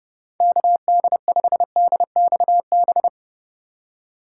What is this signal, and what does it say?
Morse code, heard as a keyed tone, about 30 wpm, 700 Hz. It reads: KD5DXB